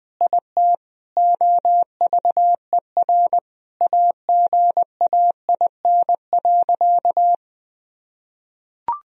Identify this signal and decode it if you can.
Morse code, heard as a keyed tone, about 21 wpm, 700 Hz. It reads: IT OVER AGAIN.
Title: Sentences from Top 1000 Words - Review Part 1 - 20wpm